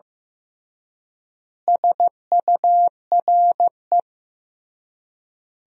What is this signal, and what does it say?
Morse code, heard as a keyed tone, about 15 wpm, 700 Hz. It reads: SURE
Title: 3x Word-Spacing - Sentences from Top 300 Words - 15wpm